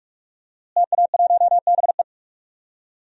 Morse code, heard as a keyed tone, about 46 wpm, 700 Hz. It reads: TA1BE
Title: DXCC-100 Call Signs - 5 Characters - Repeated 3x - 45wpm